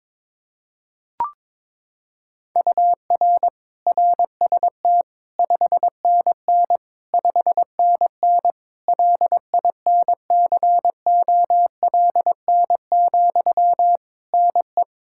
Morse code, heard as a keyed tone, about 23 wpm, 700 Hz. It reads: UR RST 5NN 5NN LINCOLN, NE
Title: Practice QSO - Exchange 1 - Beginning - 22wpm